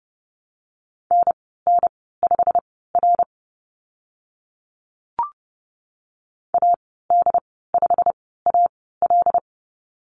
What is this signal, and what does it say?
Morse code, heard as a keyed tone, about 30 wpm, spaced as for 17 wpm, 700 Hz. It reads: NN5R AD5AL
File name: Lesson 15g - Number 5 - Call Signs 30wpm